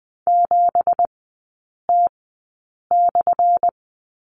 Morse code, heard as a keyed tone, about 20 wpm, spaced as for 8 wpm, 700 Hz. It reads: 7T/
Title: 178 Lesson 29e - Slash - Three Characters 20wpm